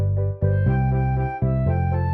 There is no code.